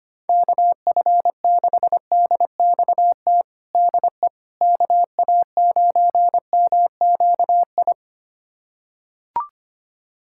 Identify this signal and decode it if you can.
Morse code, heard as a keyed tone, about 25 wpm, 700 Hz. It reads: KF6DXT DE KA9MQI
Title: USA Callsign DE Callsign - 6 Characters - 25wpm